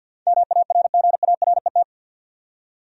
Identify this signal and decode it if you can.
Morse code, heard as a keyed tone, about 52 wpm, 700 Hz. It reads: MARGARET
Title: USA Common Names - 50wpm